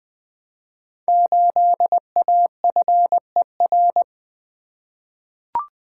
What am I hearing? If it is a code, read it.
Morse code, heard as a keyed tone, about 20 wpm, 700 Hz. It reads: 8AFER
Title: Code Groups - 5 Letter:Numbers - 20wpm